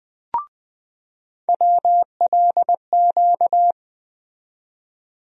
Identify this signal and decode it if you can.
Morse code, heard as a keyed tone, about 20 wpm, 700 Hz. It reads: WLQ
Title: USA Call Sign Postfixes - 3 Letters 20wpm